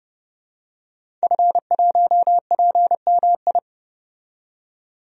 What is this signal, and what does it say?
Morse code, heard as a keyed tone, about 32 wpm, 700 Hz. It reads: F1PMI
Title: DXCC-100 Call Signs - 5 Characters - 30wpm